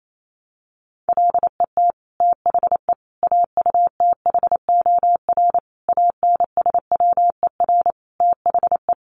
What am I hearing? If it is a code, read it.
Morse code, heard as a keyed tone, about 28 wpm, 700 Hz. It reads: LET THE AUTHOR ANSWER THE
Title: Sentences from Top 1200 Words 28wpm